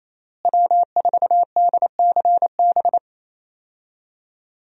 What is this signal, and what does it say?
Morse code, heard as a keyed tone, about 27 wpm, 700 Hz. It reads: W4DCB
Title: USA Call Signs - 5 Characters - Repeated 3x - 28wpm